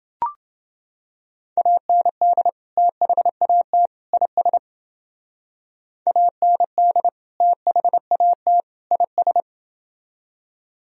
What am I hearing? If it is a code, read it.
Morse code, heard as a keyed tone, about 29 wpm, 700 Hz. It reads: AND THAT IS AND THAT IS